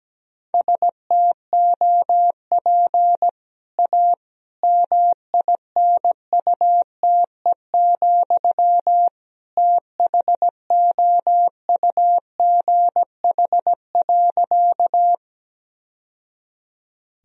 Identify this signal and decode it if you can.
Morse code, heard as a keyed tone, about 17 wpm, 700 Hz. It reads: STOP A MINUTE, THOUGH.
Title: Sentences from Top 1000 Words - Review Part 2 - 17wpm